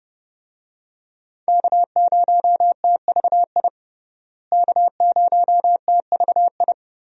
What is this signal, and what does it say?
Morse code, heard as a keyed tone, about 32 wpm, 700 Hz. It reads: K0TVI K0TVI
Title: USA Call Signs - 5 Characters - Repeated 3x - 30wpm